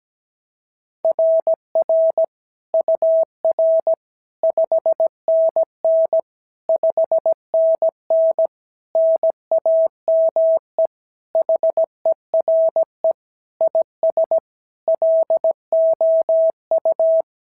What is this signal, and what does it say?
Morse code, heard as a keyed tone, about 16 wpm, 650 Hz. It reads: RR UR 5NN 5NN NAME HERE IS LOU
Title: SOTA QSO Practice 17wpm